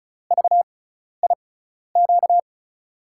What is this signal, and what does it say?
Morse code, heard as a keyed tone, about 34 wpm, spaced as for 12 wpm, 700 Hz. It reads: VIQ
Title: USA Call Signs - 6 Characters - 6x Character Spacing - 35wpm